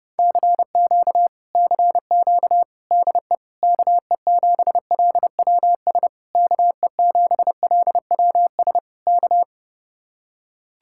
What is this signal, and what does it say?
Morse code, heard as a keyed tone, about 30 wpm, 700 Hz. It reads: CQ CQ DE KE7LWS KE7LWS K